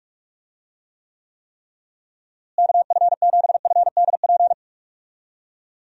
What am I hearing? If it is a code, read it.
Morse code, heard as a keyed tone, about 46 wpm, 700 Hz. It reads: KF7FDP
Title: USA Call Signs - 6 Characters - Repeated 3x - 45wpm